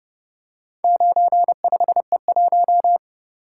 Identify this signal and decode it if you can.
Morse code, heard as a keyed tone, about 31 wpm, 700 Hz. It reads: 95E1